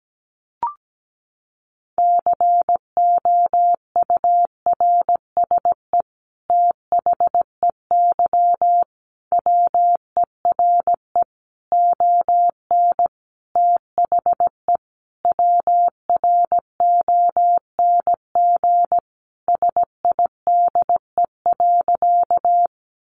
Morse code, heard as a keyed tone, about 17 wpm, 700 Hz. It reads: COURSE THEY WERE ON THE WRONG SIDE.
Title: Sentences from Top 900 Words 17wpm